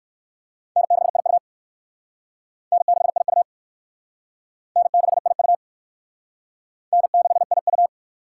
Morse code, heard as a keyed tone, about 58 wpm, 700 Hz. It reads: N6IU N6IU N6IU N6IU